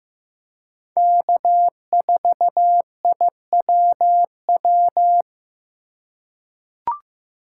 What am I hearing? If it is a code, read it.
Morse code, heard as a keyed tone, about 15 wpm, 700 Hz. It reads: K4IWW
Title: USA Call Signs - 5 Characters - 4x Character Spacing - 15wpm